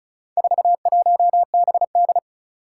Morse code, heard as a keyed tone, about 36 wpm, 700 Hz. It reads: 41BD
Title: Four Challenging Letter-Number Combinations 35wpm